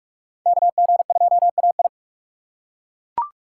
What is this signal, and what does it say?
Morse code, heard as a keyed tone, about 42 wpm, 700 Hz. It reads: KG2AI
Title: USA Call Signs - 5 Characters - 2x Character Spacing - 45wpm